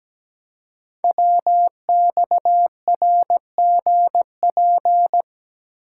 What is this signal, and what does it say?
Morse code, heard as a keyed tone, about 17 wpm, 700 Hz. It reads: WXRGP
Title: Code Groups - 5 Letters - 17wpm